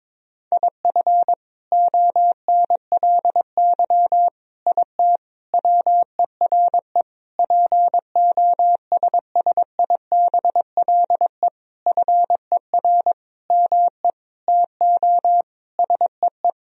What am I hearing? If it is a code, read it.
Morse code, heard as a keyed tone, about 22 wpm, 700 Hz. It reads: IF ONLY IT WERE POSSIBLE FER ME TO SEE
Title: Sentences from Top 500 Words and Abbreviations 22wpm